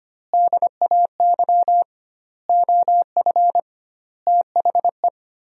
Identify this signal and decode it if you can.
Morse code, heard as a keyed tone, about 25 wpm, 700 Hz. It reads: DAY OF THE
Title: 2x Word-Spacing - Sentences from Top 300 Words - 25wpm